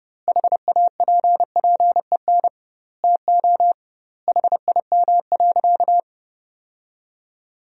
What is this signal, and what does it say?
Morse code, heard as a keyed tone, about 30 wpm, 700 Hz. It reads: HAPPEN TO HIM.
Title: 2x Word-Spacing - Sentences from Top 400 Words - 30wpm